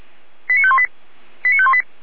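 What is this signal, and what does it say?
Five-tone selective call (ZVEI), heard as quick runs of five tones, tones of about 70 ms. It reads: 89518 89518
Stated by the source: ZVEI five-tone selective call received on SDR.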